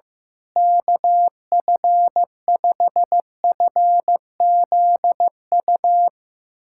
Morse code, heard as a keyed tone, about 16 wpm, 700 Hz. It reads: KF5FZU
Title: USA Callsign DE Callsign 15wpm